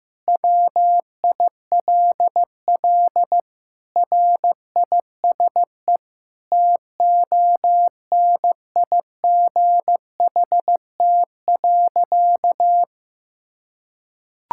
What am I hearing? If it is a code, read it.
Morse code, heard as a keyed tone, about 15 wpm, 700 Hz. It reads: WILL RISE TONIGHT.
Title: Sentences from Top 1000 Words - Review Part 6 - 15wpm